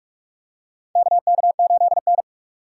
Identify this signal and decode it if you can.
Morse code, heard as a keyed tone, about 46 wpm, 700 Hz. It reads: KK8N
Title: USA Call Signs - 4 Characters - 6x Character Spacing - 45wpm